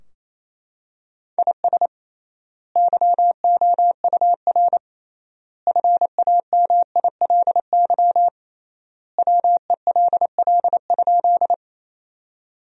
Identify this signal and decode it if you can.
Morse code, heard as a keyed tone, about 28 wpm, 700 Hz. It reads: IS YOUR FAMILY WELL?
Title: 3x Word-Spacing - Sentences from Top 400 Words - 28wpm